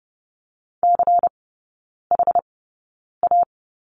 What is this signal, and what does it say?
Morse code, heard as a keyed tone, about 30 wpm, spaced as for 9 wpm, 700 Hz. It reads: CHA
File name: Lesson 16e - Letter c - Three Characters 30wpm